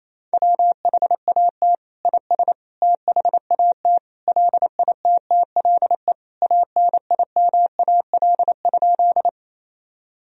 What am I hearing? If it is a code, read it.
Morse code, heard as a keyed tone, about 28 wpm, 700 Hz. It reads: WHAT IS THAT LITTLE ANIMAL?
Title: Sentences from Top 1000 Words - Review Part 1 - 28wpm